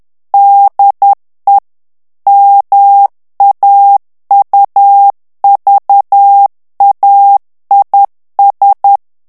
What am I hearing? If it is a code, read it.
Morse code, heard as a keyed tone, about 10 wpm, 800 Hz. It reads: DE MAUVAIS
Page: Français - texte en clair 1 - vitesse : 10 mots minute